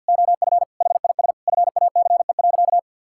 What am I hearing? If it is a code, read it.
Morse code, heard as a keyed tone, about 50 wpm, 700 Hz. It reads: OF HIS FACE.
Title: Sentences from Top 1000 Words - Review Part 5 - 50wpm